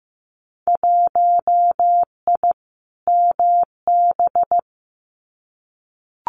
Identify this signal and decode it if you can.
Morse code, heard as a keyed tone, about 15 wpm, 700 Hz. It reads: 1I MB
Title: Winter Field Day Exchanges Encore 15wpm